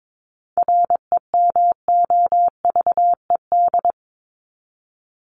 Morse code, heard as a keyed tone, about 22 wpm, 700 Hz. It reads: REMOVED
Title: ED Suffix - Top 100 Words with Base Words 22wpm